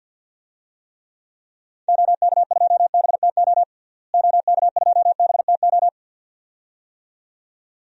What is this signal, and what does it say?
Morse code, heard as a keyed tone, about 53 wpm, 700 Hz. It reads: OK2BTO OK2BTO
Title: DXCC-100 Call Signs - 6 Characters - Repeated 2x - 50wpm